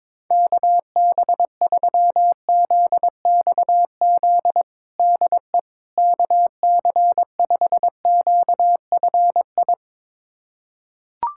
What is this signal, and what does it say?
Morse code, heard as a keyed tone, about 22 wpm, 700 Hz. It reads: KB3ZXZ DE KC5QFI
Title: USA Callsign DE Callsign - 6 Characters - 22wpm